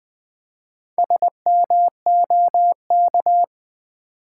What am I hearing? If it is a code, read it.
Morse code, heard as a keyed tone, about 20 wpm, 700 Hz. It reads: SMOK
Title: Build A Word - 1 Letter At A Time - 5-Letter Words - 20wpm